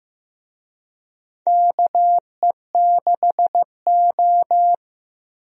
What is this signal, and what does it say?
Morse code, heard as a keyed tone, about 15 wpm, 700 Hz. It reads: KE6O
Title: USA Call Signs - 4 Characters - Repeated 3x - 15wpm